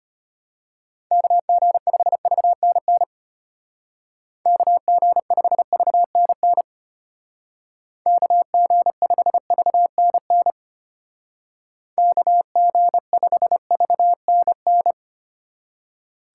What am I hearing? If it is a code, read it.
Morse code, heard as a keyed tone, about 32 wpm, 700 Hz. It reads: KG5VNN KG5VNN KG5VNN KG5VNN